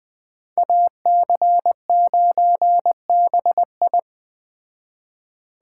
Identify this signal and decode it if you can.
Morse code, heard as a keyed tone, about 20 wpm, 700 Hz. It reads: AC9BI
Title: Speed Racing - USA 2x2 Call Signs 17wpm